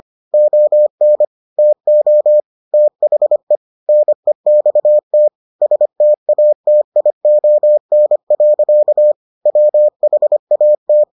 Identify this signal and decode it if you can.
Morse code, heard as a keyed tone, about 25 wpm, 600 Hz. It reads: ON TO THE NEXT STATION. WHAT